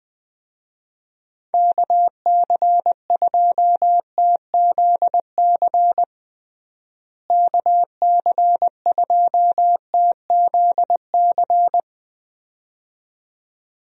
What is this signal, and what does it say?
Morse code, heard as a keyed tone, about 21 wpm, 700 Hz. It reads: KC2TZC KC2TZC
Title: USA Call Signs - 6 Characters - Repeated 2x - 20wpm